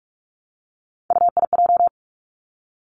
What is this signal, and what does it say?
Morse code, heard as a keyed tone, about 46 wpm, 700 Hz. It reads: UIJ
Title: Three Challenging Letter-Number Combinations 45wpm